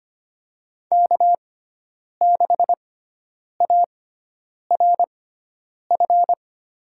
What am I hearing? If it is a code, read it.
Morse code, heard as a keyed tone, about 25 wpm, spaced as for 9 wpm, 700 Hz. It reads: K6ARF
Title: USA Call Signs - 6x Character Spacing - 25wpm